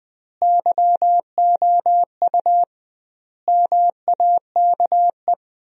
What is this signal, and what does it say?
Morse code, heard as a keyed tone, about 20 wpm, 700 Hz. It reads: YOU MAKE
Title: 2x Word-Spacing - Sentences from Top 100 Words - 20wpm